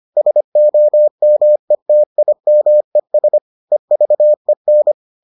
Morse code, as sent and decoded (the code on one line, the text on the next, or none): ... --- -- . - .. -- . ... / . ...- . -.
SOMETIMES EVEN